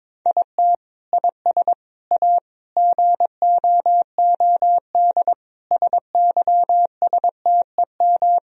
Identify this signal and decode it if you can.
Morse code, heard as a keyed tone, about 22 wpm, 700 Hz. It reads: IT IS A GOOD SYSTEM
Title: Sentences from Top 1000 Words - Review Part 3 - 22wpm